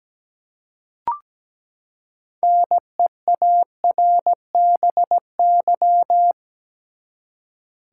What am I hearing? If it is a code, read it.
Morse code, heard as a keyed tone, about 17 wpm, 700 Hz. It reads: NEARBY
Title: Six Letter Words - 17wpm